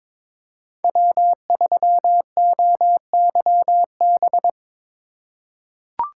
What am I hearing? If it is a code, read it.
Morse code, heard as a keyed tone, about 22 wpm, 700 Hz. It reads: W3OYB
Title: USA Call Signs - 5 Characters - 2x Character Spacing - 22wpm